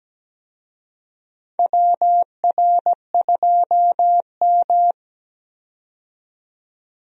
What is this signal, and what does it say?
Morse code, heard as a keyed tone, about 17 wpm, 700 Hz. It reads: WR2M
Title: USA Call Signs - 4 Characters - 17wpm